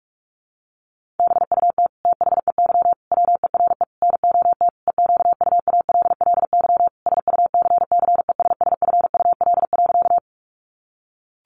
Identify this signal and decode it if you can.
Morse code, heard as a keyed tone, about 44 wpm, 700 Hz. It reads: BUT THEY WERE NOT EQUALLY SUCCESSFUL.
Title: Sentences from Top 1400 Words 45wpm